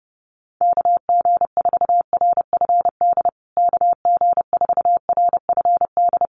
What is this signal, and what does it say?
Morse code, heard as a keyed tone, about 30 wpm, 700 Hz. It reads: KG4RFD KG4RFD
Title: USA Call Signs Repeated 3x 30wpm